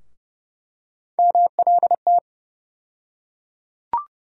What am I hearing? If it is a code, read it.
Morse code, heard as a keyed tone, about 32 wpm, 700 Hz. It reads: MLT
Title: USA Call Sign Postfixes - 3 Letters 30wpm